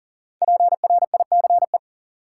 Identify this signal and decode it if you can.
Morse code, heard as a keyed tone, about 40 wpm, 700 Hz. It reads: PRICE
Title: Sentences from Top 600 Words - New Word Drill - 40wpm